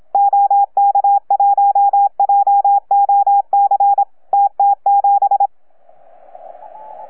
Morse code, heard as a keyed tone, about 26 wpm, 810 Hz. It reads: OK1JOC TT7
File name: UP2L-20-CW